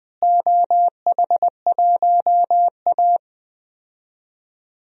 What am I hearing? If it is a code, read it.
Morse code, heard as a keyed tone, about 20 wpm, 700 Hz. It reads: OH1A
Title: DXCC-100 Call Signs - 4 Characters - Repeated 3x - 20wpm